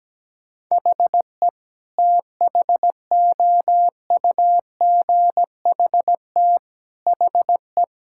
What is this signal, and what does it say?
Morse code, heard as a keyed tone, about 17 wpm, 700 Hz. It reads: HE THOUGHT HE
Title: Sentences from Top 1000 Words - Review Part 6 - 17wpm